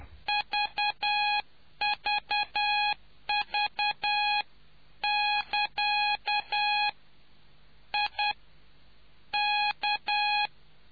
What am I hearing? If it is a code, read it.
Morse code, heard as a keyed tone, about 10 wpm, 830 Hz. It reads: VVV<KA> I K